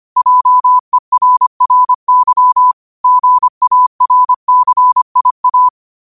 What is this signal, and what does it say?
Morse code, heard as a keyed tone, about 25 wpm, 1000 Hz. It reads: JERRY GARCIA
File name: 25wpm